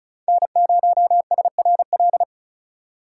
Morse code, heard as a keyed tone, about 34 wpm, 700 Hz. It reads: N0SRL
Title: USA Call Signs - 5 Characters - 4x Character Spacing - 35wpm